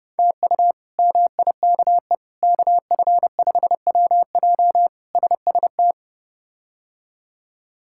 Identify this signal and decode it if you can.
Morse code, heard as a keyed tone, about 30 wpm, 700 Hz. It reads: TU MIKE KF5WJ SST
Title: K1USN SST QSO Practice - 30wpm